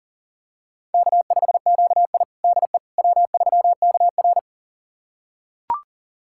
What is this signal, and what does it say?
Morse code, heard as a keyed tone, about 40 wpm, 700 Hz. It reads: K5QI DE W3KR
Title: USA Callsign DE Callsign - 4 Characters - 40wpm